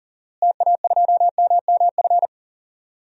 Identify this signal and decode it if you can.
Morse code, heard as a keyed tone, about 39 wpm, 700 Hz. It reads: TA2MMF
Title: DXCC-100 Call Signs - 6 Characters - Repeated 3x - 40wpm